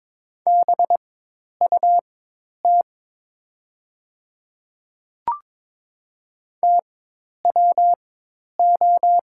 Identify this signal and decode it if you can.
Morse code, heard as a keyed tone, about 22 wpm, spaced as for 10 wpm, 700 Hz. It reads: BUT TWO
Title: 4x Character Spacing - Top 200 Words - 22wpm